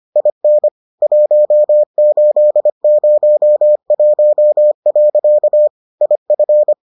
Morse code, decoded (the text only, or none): IN 1801. IF